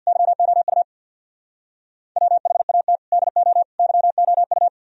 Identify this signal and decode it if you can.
Morse code, heard as a keyed tone, about 48 wpm, 700 Hz. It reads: YOU WHAT DO YOU
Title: Speed-Racing - Sets of 3 Words 50wpm